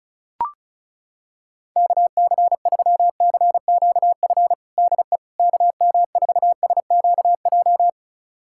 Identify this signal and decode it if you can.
Morse code, heard as a keyed tone, about 36 wpm, 700 Hz. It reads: KC3CQF DE KM4SQJ
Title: USA Callsign DE Callsign - 6 Characters - 35wpm